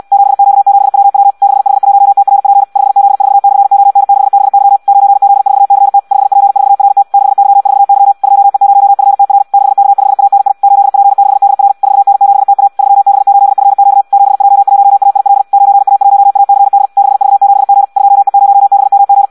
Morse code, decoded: 7Q7WW 6H1TRP 5Z4/ON6BY 9Z4CT 5P4RT 6Y4Y 2E0FTR 6K5ANI 8P6DR 4N7TA 3B8FQ 7J1AIL 9A1ACD 6V7G 2E0BMG